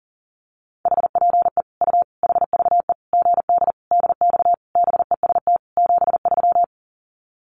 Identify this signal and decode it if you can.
Morse code, heard as a keyed tone, about 40 wpm, 700 Hz. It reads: HPE U HVE GD DX BEST 73
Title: Common QSO Phrases 40wpm